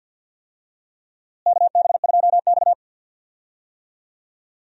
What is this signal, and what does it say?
Morse code, heard as a keyed tone, about 52 wpm, 700 Hz. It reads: KB2X
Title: USA Call Signs - 4 Characters - Repeated 3x - 50wpm